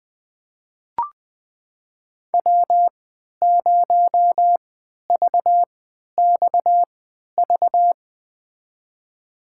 Morse code, heard as a keyed tone, about 20 wpm, spaced as for 11 wpm, 700 Hz. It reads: W0VXV
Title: USA Call Signs - 5 Characters - 3x Character Spacing - 20wpm